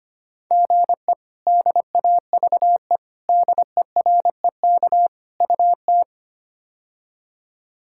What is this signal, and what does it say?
Morse code, heard as a keyed tone, about 25 wpm, 700 Hz. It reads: GE DAVE DEREK UT
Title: K1USN SST QSO Practice - 25wpm